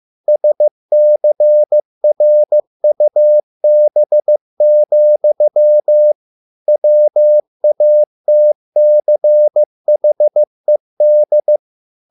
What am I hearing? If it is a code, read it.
Morse code, heard as a keyed tone, about 15 wpm, 600 Hz. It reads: SCRUB, WATCHED